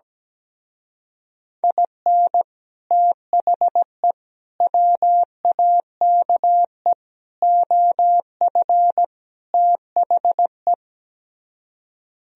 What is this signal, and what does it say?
Morse code, heard as a keyed tone, about 17 wpm, 700 Hz. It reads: IN THE WAKE OF THE
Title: Speed-Racing - Sets of 5 Words 17wpm